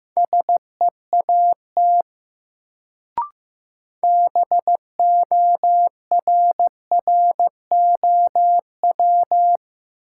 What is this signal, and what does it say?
Morse code, heard as a keyed tone, about 15 wpm, 700 Hz. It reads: SEAT BORROW
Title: Sentences from Top 1500 Words - New Word Drill Alt - 15wpm